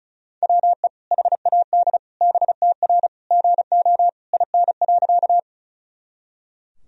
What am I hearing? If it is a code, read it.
Morse code, heard as a keyed tone, about 34 wpm, 700 Hz. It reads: WE HAD BTR GO IN.
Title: Sentences from Top 400 Words and Abbreviations 35wpm